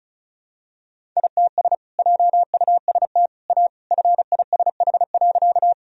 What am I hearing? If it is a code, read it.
Morse code, heard as a keyed tone, about 36 wpm, 700 Hz. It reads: ITS JUST A FISH.
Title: Sentences from Top 1000 Words - Review Part 1 - 35wpm